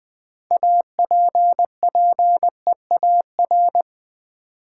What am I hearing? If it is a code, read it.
Morse code, heard as a keyed tone, about 20 wpm, 700 Hz. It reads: APPEAR
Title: Sentences from Top 1000 Words - Review Part 2 - New Word Drill - 20wpm